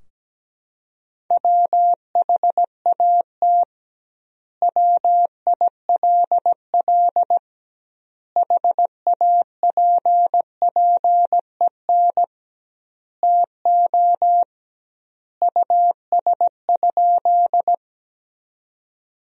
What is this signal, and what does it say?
Morse code, heard as a keyed tone, about 17 wpm, 700 Hz. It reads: WHAT WILL HAPPEN TO US?
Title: 2x Word-Spacing - Sentences from Top 400 Words - 17wpm